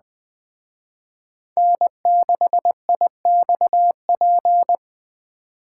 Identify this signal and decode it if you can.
Morse code, heard as a keyed tone, about 20 wpm, 700 Hz. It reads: N6IXP
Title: Speed Racing - USA 1x3 Call Signs 17wpm